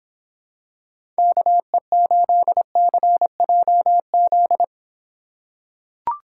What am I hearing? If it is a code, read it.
Morse code, heard as a keyed tone, about 26 wpm, 700 Hz. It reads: KE8CJZ